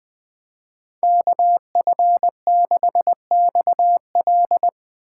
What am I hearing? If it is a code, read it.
Morse code, heard as a keyed tone, about 21 wpm, 700 Hz. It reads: KF6XL